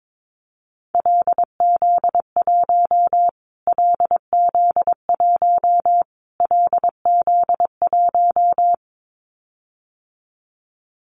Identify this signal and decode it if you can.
Morse code, heard as a keyed tone, about 22 wpm, 700 Hz. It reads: LZ1 LZ1 LZ1
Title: Callsign Prefixes for Top 25 DXCC - Repeated 3x - 22wpm